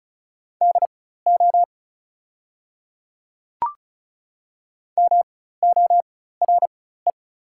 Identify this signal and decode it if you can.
Morse code, heard as a keyed tone, about 36 wpm, spaced as for 16 wpm, 700 Hz. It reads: DO MORE
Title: 4x Character Spacing - Top 100 Words - 35wpm